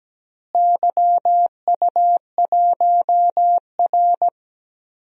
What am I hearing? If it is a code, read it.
Morse code, heard as a keyed tone, about 17 wpm, 700 Hz. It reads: YU1R
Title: DXCC-100 Call Signs - 4 Characters - Repeated 2x - 17wpm